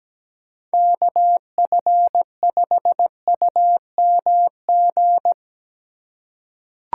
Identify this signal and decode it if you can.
Morse code, heard as a keyed tone, about 17 wpm, 700 Hz. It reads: KF5UMG
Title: USA Call Signs - 6 Characters - 3x Character Spacing - 17wpm